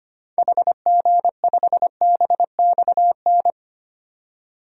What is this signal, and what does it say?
Morse code, heard as a keyed tone, about 25 wpm, 700 Hz. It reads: HG5BXN
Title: DXCC-100 Call Signs - 6 Characters - 25wpm